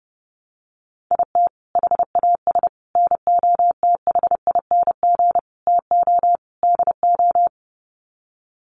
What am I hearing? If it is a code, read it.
Morse code, heard as a keyed tone, about 30 wpm, 700 Hz. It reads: IT HAS NOTHING TO DO
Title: Sets of 5 Words Encore 30wpm